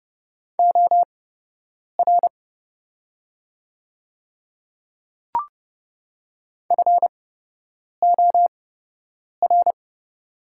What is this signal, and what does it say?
Morse code, heard as a keyed tone, about 30 wpm, spaced as for 8 wpm, 700 Hz. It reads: OR FOR
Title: 8x Character Spacing - Top 200 Words - 30wpm